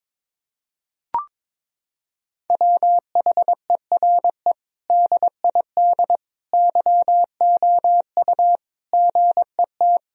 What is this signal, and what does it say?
Morse code, heard as a keyed tone, about 22 wpm, 700 Hz. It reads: WHERE DID YOU GET
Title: Sentences from Top 1000 Words - Review Part 4 - 22wpm